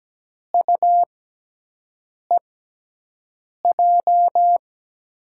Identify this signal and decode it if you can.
Morse code, heard as a keyed tone, about 17 wpm, spaced as for 6 wpm, 700 Hz. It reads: UEJ
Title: USA Call Signs - 6x Character Spacing - 17wpm